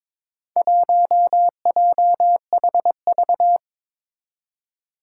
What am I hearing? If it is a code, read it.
Morse code, heard as a keyed tone, about 21 wpm, 700 Hz. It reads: 1JHV
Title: Four Challenging Letter-Number Combinations 22wpm